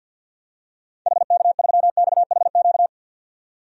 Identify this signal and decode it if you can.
Morse code, heard as a keyed tone, about 52 wpm, 700 Hz. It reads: HK3XHQ
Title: DXCC-100 Call Signs - 6 Characters - 50wpm